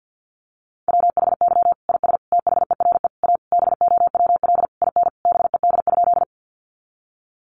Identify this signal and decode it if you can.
Morse code, heard as a keyed tone, about 50 wpm, 700 Hz. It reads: WHY IS THERE A BOWL IN BED?
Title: Sentences from Top 1200 Words 50wpm